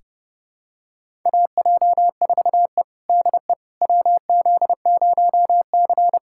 Morse code, heard as a keyed tone, about 32 wpm, 700 Hz. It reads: AJ4E DE WZ0C
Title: USA Callsign DE Callsign - 4 Characters - 30wpm